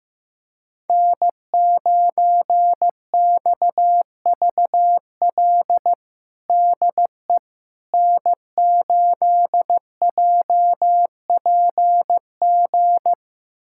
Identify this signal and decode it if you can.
Morse code, heard as a keyed tone, about 16 wpm, 700 Hz. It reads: N9XVL DE N8JPG
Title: USA Callsign DE Callsign - 5 Characters - 15wpm